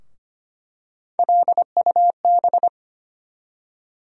Morse code, heard as a keyed tone, about 25 wpm, 700 Hz. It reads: LUB
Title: USA Call Sign Postfixes - 3 Letters 25wpm